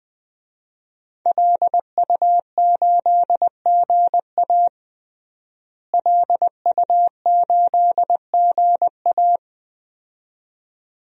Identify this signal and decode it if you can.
Morse code, heard as a keyed tone, about 20 wpm, 700 Hz. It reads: LU8GA LU8GA
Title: DXCC-100 Call Signs - 5 Characters - Repeated 2x - 20wpm